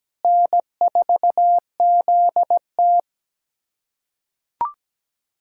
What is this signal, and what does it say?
Morse code, heard as a keyed tone, about 17 wpm, 700 Hz. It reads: N4ZT
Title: USA Callsign DE Callsign - 4 Characters - 17wpm